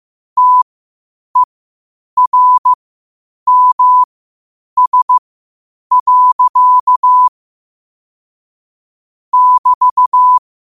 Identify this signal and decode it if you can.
Morse code, heard as a keyed tone, about 15 wpm, spaced as for 8 wpm, 1000 Hz. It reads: TERMS. <BT>